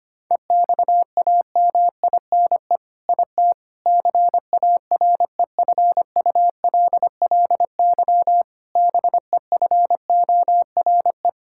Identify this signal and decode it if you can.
Morse code, heard as a keyed tone, about 25 wpm, 700 Hz. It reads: EXAMINE IT CAREFULLY BEFORE
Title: Sentences from Top 1600 Words 25wpm